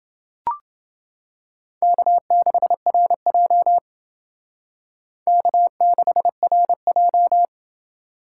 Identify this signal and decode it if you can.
Morse code, heard as a keyed tone, about 29 wpm, 700 Hz. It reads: K6RJ K6RJ